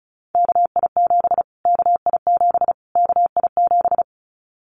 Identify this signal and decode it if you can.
Morse code, heard as a keyed tone, about 34 wpm, 700 Hz. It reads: KI7 KI7 KI7
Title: Callsign Prefixes for Top 25 DXCC - Repeated 3x - 35wpm